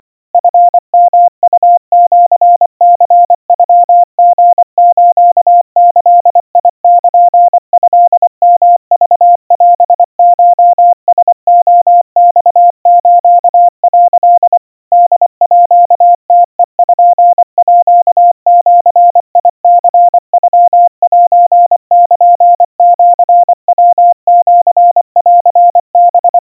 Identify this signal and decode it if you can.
Morse code (Japanese Wabun code), heard as a keyed tone, about 24 wpm, 700 Hz. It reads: チヨウシニノリスギルトヨクオコラレマス」 ホームページニノセルシヤシンハ
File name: 90 cpm 700 Hz Crispy